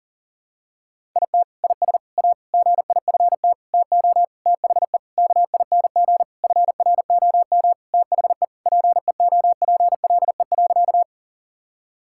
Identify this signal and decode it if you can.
Morse code, heard as a keyed tone, about 40 wpm, 700 Hz. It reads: IT IS A GIFT TO THE KING FROM THE PEOPLE.
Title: Sentences from Top 1000 Words - Review Part 6 - 40wpm